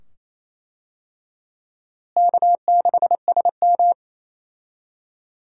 Morse code, heard as a keyed tone, about 27 wpm, 700 Hz. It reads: K6SM
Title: USA Call Signs - 4 Characters - Repeated 2x - 28wpm